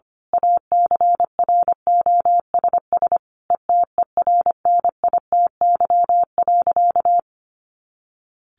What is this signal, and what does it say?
Morse code, heard as a keyed tone, about 25 wpm, 700 Hz. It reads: ACROSS ETERNITY.
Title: A Sound Of Thunder 25wpm